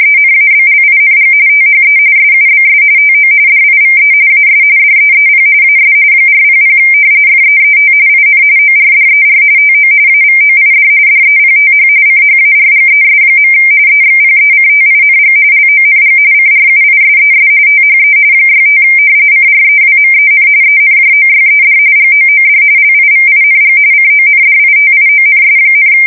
RTTY, about 45.45 baud. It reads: ENABLING THE SUBS TO DIVE AS DEEP AS 400 FEET IN ORDER TO AVOID DEPTH CHARGES. EXPECTING LONG CRUISES THE 65 CREWMEN ENJOYED GOOD LIVING CONDITIONS,
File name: fsk170 / usnavy4